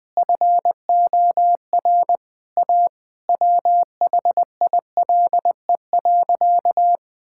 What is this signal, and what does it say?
Morse code, heard as a keyed tone, about 20 wpm, 700 Hz. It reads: FOR A WHILE.
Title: Sentences with 3-Word Phrases - 20wpm